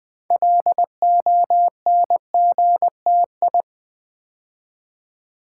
Morse code, heard as a keyed tone, about 20 wpm, 700 Hz. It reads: LONGTI
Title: Build A Word - 2 Letters At A Time - 8-Letter Words - 20wpm